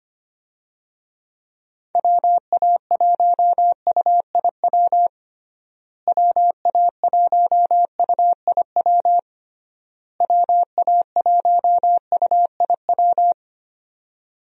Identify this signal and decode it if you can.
Morse code, heard as a keyed tone, about 25 wpm, 700 Hz. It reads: WA1UIW WA1UIW WA1UIW
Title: USA Call Signs - 6 Characters - Repeated 3x - 25wpm